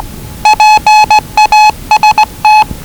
Morse code, heard as a keyed tone, about 20 wpm, 840 Hz. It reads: PAST
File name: WellingtonRadioZlwCloseDownOnHfAndMf / 01-Zlw 4277khz 417.5 qrt until 0925z